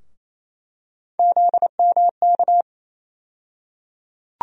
Morse code, heard as a keyed tone, about 28 wpm, 700 Hz. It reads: ZMK
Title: USA Call Sign Postfixes - 3 Letters 28wpm